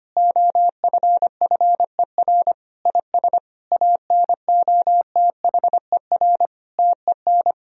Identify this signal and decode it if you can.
Morse code, heard as a keyed tone, about 25 wpm, 700 Hz. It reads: OFFER IS ANOTHER TEN
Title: Sentences from Top 1000 Words - Review Part 4 - 25wpm